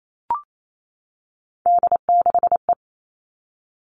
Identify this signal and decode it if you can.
Morse code, heard as a keyed tone, about 28 wpm, 700 Hz. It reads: D6E
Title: All Cut Numbers - 3-Digit 28wpm